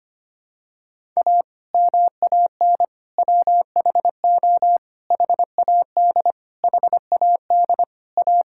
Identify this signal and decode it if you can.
Morse code, heard as a keyed tone, about 25 wpm, 700 Hz. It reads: A MAN WHO HAD HAD A
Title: Sentences from Top 1000 Words - Review Part 6 - 25wpm